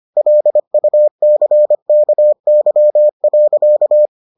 Morse code, heard as a keyed tone, about 25 wpm, 600 Hz. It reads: LUCKY.